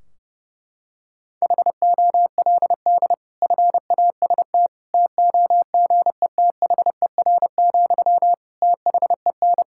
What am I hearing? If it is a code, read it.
Morse code, heard as a keyed tone, about 30 wpm, 700 Hz. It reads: HOLD FAST TOGETHER, THEN
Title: Sentences from Top 1600 Words 30wpm